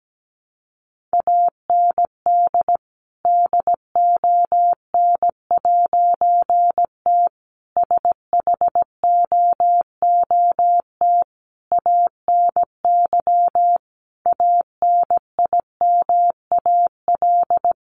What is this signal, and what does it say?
Morse code, heard as a keyed tone, about 17 wpm, 700 Hz. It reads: AND DON'T SHOOT ANY ANIMAL
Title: A Sound Of Thunder 17wpm